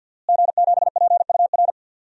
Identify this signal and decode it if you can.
Morse code, heard as a keyed tone, about 50 wpm, 700 Hz. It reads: G7PUR
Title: DXCC-100 Call Signs - 5 Characters - Repeated 3x - 50wpm